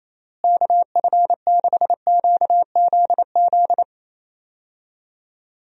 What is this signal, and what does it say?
Morse code, heard as a keyed tone, about 27 wpm, 700 Hz. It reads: KF6QZZ
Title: USA Call Signs - 6 Characters - Repeated 2x - 28wpm